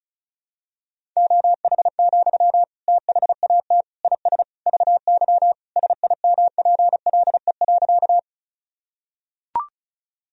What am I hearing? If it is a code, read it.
Morse code, heard as a keyed tone, about 34 wpm, 700 Hz. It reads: OH, THAT IS VY SIMPLE.
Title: Sentences from Top 500 Words and Abbreviations 35wpm